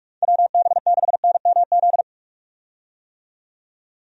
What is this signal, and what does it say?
Morse code, heard as a keyed tone, about 47 wpm, 700 Hz. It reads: WB6NMZ
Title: USA Call Signs - 6 Characters - 45wpm